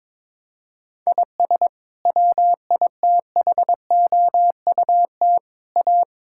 Morse code, heard as a keyed tone, about 22 wpm, 700 Hz. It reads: IS WITHOUT A
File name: practice / 22wpm / Sets of 7 Words - 22wpm